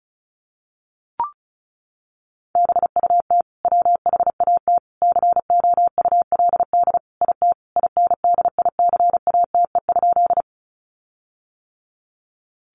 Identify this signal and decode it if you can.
Morse code, heard as a keyed tone, about 34 wpm, 700 Hz. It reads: BUT WHAT COULD IT INDICATE?
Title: Sentences from Top 1300 Words 35wpm